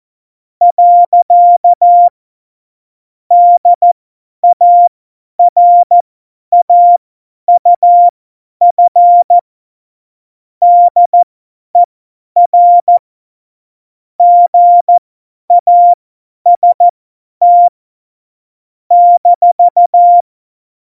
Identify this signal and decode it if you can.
Morse code, heard as a keyed tone, about 14 wpm, spaced as for 10 wpm, 700 Hz. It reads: . DARAUF DER GAST -
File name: Morsekurs / Witze / Witz 07. Januar 2020, 50 F70